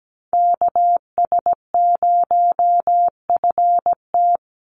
Morse code, heard as a keyed tone, about 17 wpm, 700 Hz. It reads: KS0FT
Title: USA Call Signs 17wpm